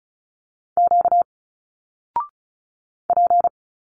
Morse code, heard as a keyed tone, about 35 wpm, 700 Hz. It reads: Q P
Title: Single Letters 35wpm